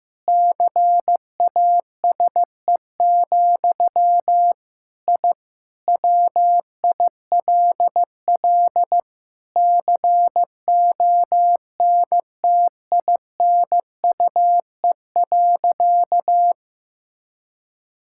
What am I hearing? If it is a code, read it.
Morse code, heard as a keyed tone, about 15 wpm, 700 Hz. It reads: CASE, I WILL CONTINUE.
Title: Sentences from Top 1000 Words - Review Part 6 - 15wpm